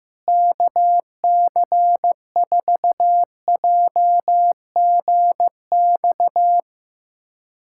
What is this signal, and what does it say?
Morse code, heard as a keyed tone, about 15 wpm, 700 Hz. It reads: KC4JGX
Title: USA Callsign DE Callsign - 6 Characters - 15wpm